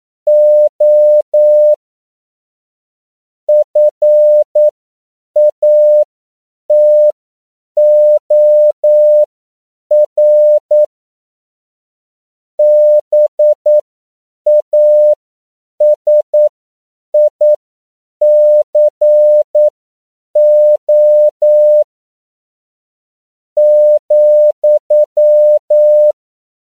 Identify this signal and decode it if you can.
Morse code, heard as a keyed tone, about 9 wpm, spaced as for 7 wpm, 600 Hz. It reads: O FATOR BASICO ,
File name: CW / 10PPM